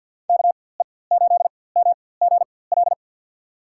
Morse code, heard as a keyed tone, about 47 wpm, spaced as for 23 wpm, 700 Hz. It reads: KE8MGL
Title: USA Call Signs - 6 Characters - 4x Character Spacing - 50wpm